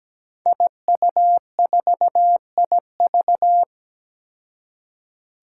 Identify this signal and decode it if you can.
Morse code, heard as a keyed tone, about 17 wpm, 700 Hz. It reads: IU4IV
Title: Five Challenging Letter-Number Combinations 17wpm